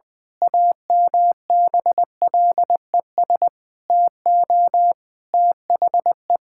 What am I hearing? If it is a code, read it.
Morse code, heard as a keyed tone, about 20 wpm, 700 Hz. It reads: AMBLES TO THE
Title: Pangram Challenge v2 - 20wpm